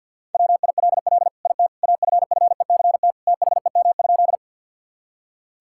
Morse code, heard as a keyed tone, about 50 wpm, 700 Hz. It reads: WILL IT AFFECT THEM?
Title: Sentences from Top 1000 Words - Review Part 6 - 50wpm